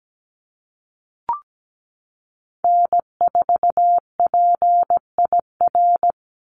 Morse code, heard as a keyed tone, about 17 wpm, 700 Hz. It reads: N4PIR
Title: Call Signs Repeated 2x - 17wpm